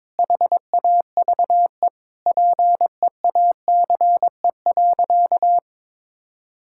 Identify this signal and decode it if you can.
Morse code, heard as a keyed tone, about 22 wpm, 700 Hz. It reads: HAVE PEACE.
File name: Sentences with 9-Word Phrases - 22wpm